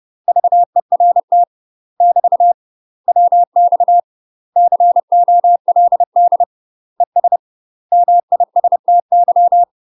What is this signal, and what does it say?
Morse code, heard as a keyed tone, about 30 wpm, 700 Hz. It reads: VERT <BT> WX COLD ES MISTY
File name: UK MASTER QSOs TRAINING / 30WPM / UK MASTER QSOs 0014